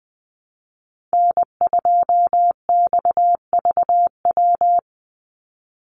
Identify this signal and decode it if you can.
Morse code, heard as a keyed tone, about 20 wpm, 700 Hz. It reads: N2XVW
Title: USA Call Signs 20wpm